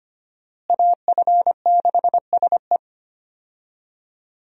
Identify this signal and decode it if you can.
Morse code, heard as a keyed tone, about 25 wpm, 700 Hz. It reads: AF6SE
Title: USA Call Signs - 5 Characters - Repeated 3x - 25wpm